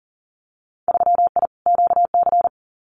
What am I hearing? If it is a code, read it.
Morse code, heard as a keyed tone, about 37 wpm, 700 Hz. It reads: 3I QC
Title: Winter Field Day Exchanges Encore 40wpm